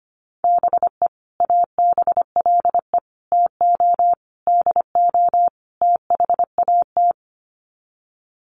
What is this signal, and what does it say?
Morse code, heard as a keyed tone, about 25 wpm, 700 Hz. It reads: BE ABLE TO DO THAT
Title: Sets of 5 Words Encore2 25wpm